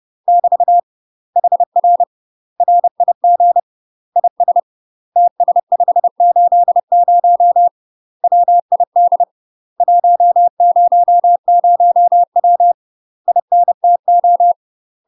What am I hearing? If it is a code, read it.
Morse code, heard as a keyed tone, about 30 wpm, 700 Hz. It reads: <BT> HR RIG IS TS580 WID 100W INTO
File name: UK MASTER QSOs TRAINING / 30WPM / UK MASTER QSOs 0035